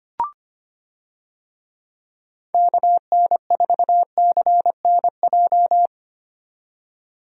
Morse code, heard as a keyed tone, about 25 wpm, 700 Hz. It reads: KN4CNJ
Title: USA Call Signs - 6 Characters - 25wpm